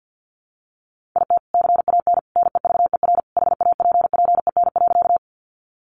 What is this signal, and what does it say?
Morse code, heard as a keyed tone, about 50 wpm, 700 Hz. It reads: IT CAN NEVER HAPPEN.
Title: Sentences from Top 400 Words 50wpm